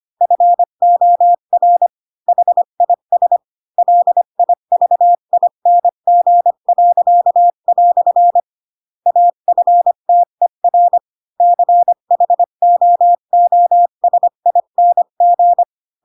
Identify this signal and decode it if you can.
Morse code, heard as a keyed tone, about 25 wpm, 700 Hz. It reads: FOR HIS LIVING." AFTER CHOOSING
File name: Pinocchio / Pinocchio ch3 20 25 ik1hge